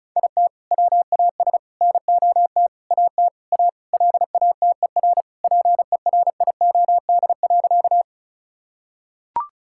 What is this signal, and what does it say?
Morse code, heard as a keyed tone, about 36 wpm, 700 Hz. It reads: IT WAS NOT AT A LATER PERIOD.
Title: Sentences from Top 1000 Words - Review Part 5 - 35wpm